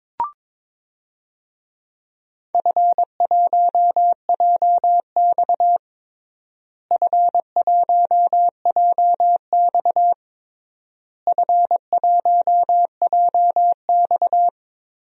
Morse code, heard as a keyed tone, about 22 wpm, 700 Hz. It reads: F1JX F1JX F1JX
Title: DXCC-100 Call Signs - 4 Characters - Repeated 3x - 22wpm